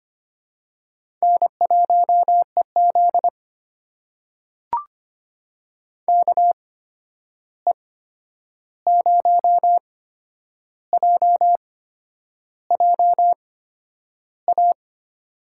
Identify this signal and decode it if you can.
Morse code, heard as a keyed tone, about 25 wpm, spaced as for 9 wpm, 700 Hz. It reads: N1EZ KE0JJA
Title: USA Call Signs - 8x Character Spacing - 25wpm